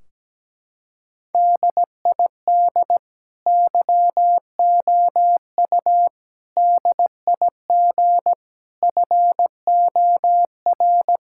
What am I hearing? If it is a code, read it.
Morse code, heard as a keyed tone, about 17 wpm, 700 Hz. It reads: DID YOU DIG FOR
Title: Sentences from Top 1600 Words 17wpm